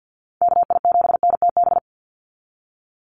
Morse code, heard as a keyed tone, about 50 wpm, 700 Hz. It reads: KI7NTB
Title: USA Call Signs 50wpm